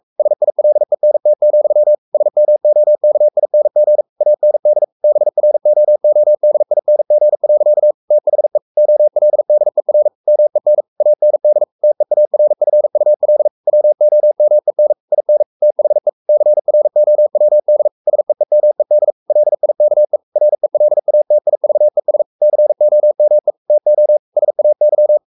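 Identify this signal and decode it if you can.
Morse code, heard as a keyed tone, about 44 wpm, 600 Hz. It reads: SILENT, SMOKING AND BROODING. THE OLDER MEN AND TEARFUL WOMEN IN THE CROWD SEEMED LIKE RELATIVES COME TO SAY
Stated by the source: Morse code at 43 wpm sending Ken Follett's novel Never in English, from CW med Gnister.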